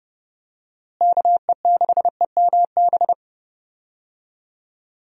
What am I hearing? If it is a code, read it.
Morse code, heard as a keyed tone, about 29 wpm, 700 Hz. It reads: KE6EMB
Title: USA Call Signs - 6 Characters - Repeated 2x - 30wpm